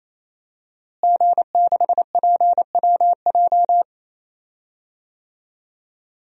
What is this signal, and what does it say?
Morse code, heard as a keyed tone, about 28 wpm, 700 Hz. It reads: G6PWJ
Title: DXCC-100 Call Signs - 5 Characters - Repeated 3x - 28wpm